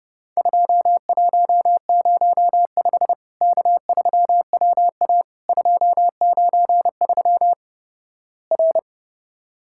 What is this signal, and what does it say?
Morse code, heard as a keyed tone, about 30 wpm, 700 Hz. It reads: 2105 K3WA 293 R
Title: Worked All Europe DX Contest - QTC Exchange - 30wpm